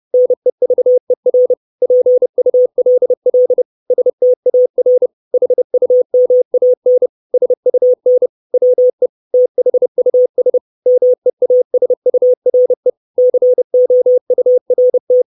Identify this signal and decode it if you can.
Morse code, heard as a keyed tone, about 30 wpm, 500 Hz. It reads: NEVER PULL STAR HUMAN SUN WE THUS MEASURE COURT
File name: 632 Words 30WPM